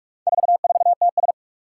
Morse code, heard as a keyed tone, about 44 wpm, 700 Hz. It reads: 44TS
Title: Four Challenging Letter-Number Combinations 45wpm